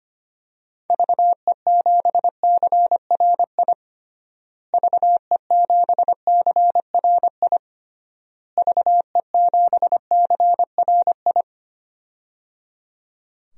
Morse code, heard as a keyed tone, about 25 wpm, 700 Hz. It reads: VE7CRI VE7CRI VE7CRI
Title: DXCC-100 Call Signs - 6 Characters - Repeated 3x - 25wpm